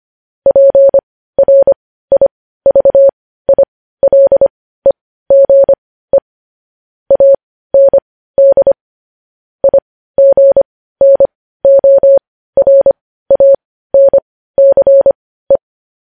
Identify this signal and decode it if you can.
Morse code, heard as a keyed tone, about 25 wpm, spaced as for 15 wpm, 550 Hz. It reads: PRIVILEGE AND IGNORANCE
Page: news-headlines-in-morse-code-at-15-wpm